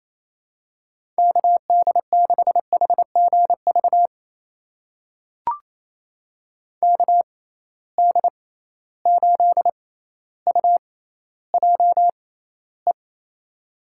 Morse code, heard as a keyed tone, about 28 wpm, spaced as for 14 wpm, 700 Hz. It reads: KD6HGV KD8UJE
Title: USA Call Signs - 6 Characters - 6x Character Spacing - 28wpm